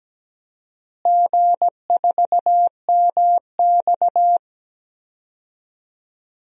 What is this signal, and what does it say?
Morse code, heard as a keyed tone, about 17 wpm, 700 Hz. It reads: G4MX
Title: DXCC-100 Call Signs - 4 Characters - Repeated 3x - 17wpm